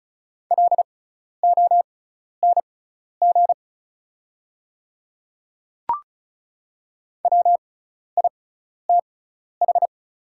Morse code, heard as a keyed tone, about 34 wpm, spaced as for 12 wpm, 700 Hz. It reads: LONG WITH
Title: 6x Character Spacing - Top 300 Words - 35wpm